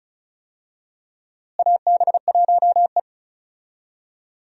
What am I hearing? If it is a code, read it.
Morse code, heard as a keyed tone, about 36 wpm, 700 Hz. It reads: AB1E
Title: USA Call Signs - 4 Characters - Repeated 3x - 35wpm